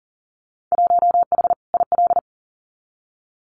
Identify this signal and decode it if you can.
Morse code, heard as a keyed tone, about 40 wpm, 700 Hz. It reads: 1H IL
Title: Winter Field Day Exchanges 40wpm